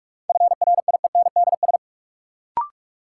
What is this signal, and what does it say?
Morse code, heard as a keyed tone, about 44 wpm, 700 Hz. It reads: FRIENDS
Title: Sentences from Top 1600 Words - New Word Drill Alt - 45wpm